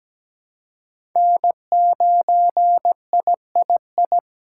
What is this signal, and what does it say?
Morse code, heard as a keyed tone, about 17 wpm, 700 Hz. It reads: N9III
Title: USA Call Signs - 2x Character Spacing - 17wpm